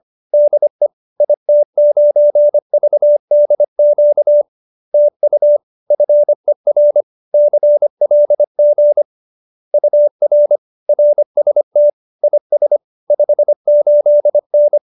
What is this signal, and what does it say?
Morse code, heard as a keyed tone, about 25 wpm, 590 Hz. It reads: DE IT9VDQ TU FER CLG UR RST IS 58N